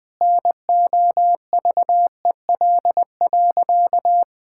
Morse code, heard as a keyed tone, about 20 wpm, 700 Hz. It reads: NOVEL.
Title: Sentences with 4-Word Phrases - Spoken Beginning - 20wpm